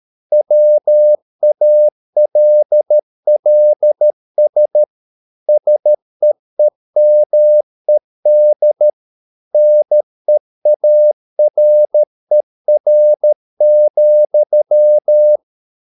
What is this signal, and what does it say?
Morse code, heard as a keyed tone, about 13 wpm, 600 Hz. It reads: WALLS SEEMED NEARER,
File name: Never 0013wpm